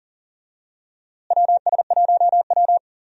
Morse code, heard as a keyed tone, about 40 wpm, 700 Hz. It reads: WS1W
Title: USA Call Signs - 4 Characters - Repeated 2x - 40wpm